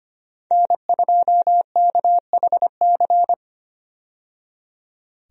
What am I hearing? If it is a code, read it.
Morse code, heard as a keyed tone, about 25 wpm, 700 Hz. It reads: N2KHC
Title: USA Call Signs - 5 Characters - 25wpm